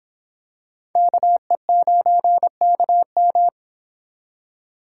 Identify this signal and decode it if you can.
Morse code, heard as a keyed tone, about 26 wpm, 700 Hz. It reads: KE9KM